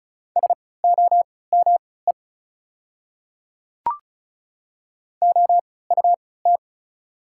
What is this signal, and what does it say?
Morse code, heard as a keyed tone, about 35 wpm, spaced as for 20 wpm, 690 Hz. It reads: SOME OUT
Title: 3x Character Spacing - Top 100 Words - 35wpm